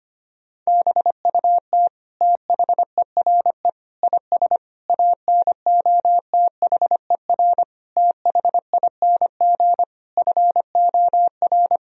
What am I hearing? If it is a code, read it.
Morse code, heard as a keyed tone, about 25 wpm, 700 Hz. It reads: BUT THERE IS ANOTHER THING FOR
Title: Sentences from Top 1000 Words - Review Part 4 - 25wpm